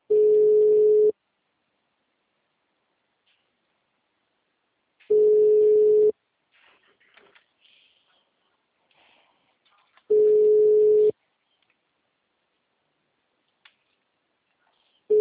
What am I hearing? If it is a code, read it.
Ringback tone (Europe, 425 Hz).